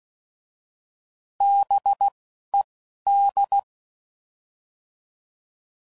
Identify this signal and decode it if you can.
Morse code, heard as a keyed tone, about 16 wpm, 800 Hz. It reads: B E D